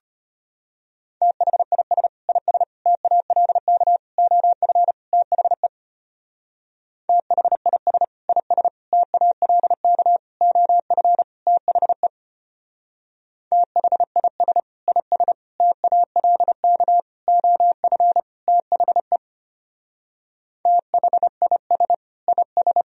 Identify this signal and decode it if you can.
Morse code, heard as a keyed tone, about 32 wpm, 700 Hz. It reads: THIS IS TALK OF THE THIS IS TALK OF THE THIS IS TALK OF THE THIS IS